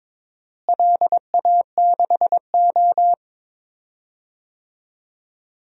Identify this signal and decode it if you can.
Morse code, heard as a keyed tone, about 22 wpm, 700 Hz. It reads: LA6O
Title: DXCC-100 Call Signs - 4 Characters - Repeated 3x - 22wpm